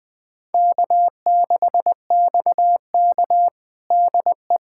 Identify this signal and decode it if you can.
Morse code, heard as a keyed tone, about 20 wpm, 700 Hz. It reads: K6XK DE
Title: USA Callsign DE Callsign - 4 Characters - 20wpm